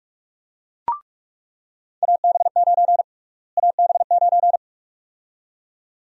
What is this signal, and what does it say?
Morse code, heard as a keyed tone, about 46 wpm, 700 Hz. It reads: AB9 AB9
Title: USA Call Sign Prefixes - Repeated 2x - 45wpm